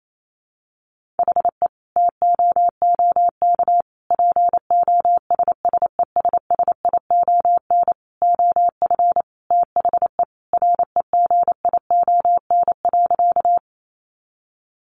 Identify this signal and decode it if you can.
Morse code, heard as a keyed tone, about 28 wpm, 700 Hz. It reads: HE TOOK POSSESSION OF THE REGION.
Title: Sentences from Top 1500 Words 28wpm